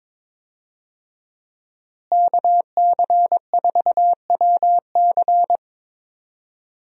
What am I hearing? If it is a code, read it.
Morse code, heard as a keyed tone, about 22 wpm, 700 Hz. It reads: KC4WC
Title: USA Call Signs - 5 Characters - Repeated 3x - 22wpm